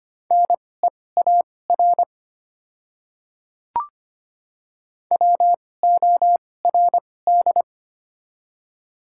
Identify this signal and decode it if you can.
Morse code, heard as a keyed tone, about 25 wpm, spaced as for 18 wpm, 700 Hz. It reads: NEAR WORD
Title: 2x Character Spacing - Top 200 Words - 25wpm